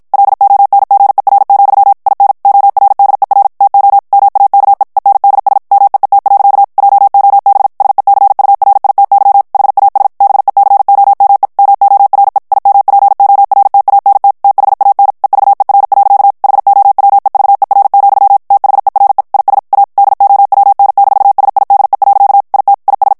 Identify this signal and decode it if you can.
Morse code, heard as a keyed tone, about 51 wpm, 800 Hz. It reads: LONGER, IN ORDER TO MAKE ENDS MEET. JOB SECURITY HAS BECOME MORE IMPORTANT THAN EVER. HOWEVER, THERE IS A DOWN-SIDE. IT IS
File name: morse mp3 / ANGLAIS texte clair 1 vitesse 50 mots minute